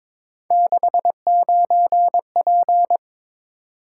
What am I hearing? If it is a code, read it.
Morse code, heard as a keyed tone, about 22 wpm, 700 Hz. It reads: 69P
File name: practice / 22wpm / Three Letter-Number Combinations - 22wpm